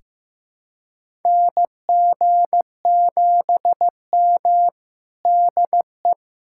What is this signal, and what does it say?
Morse code, heard as a keyed tone, about 15 wpm, 700 Hz. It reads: NG7M DE
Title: USA Callsign DE Callsign - 4 Characters - 15wpm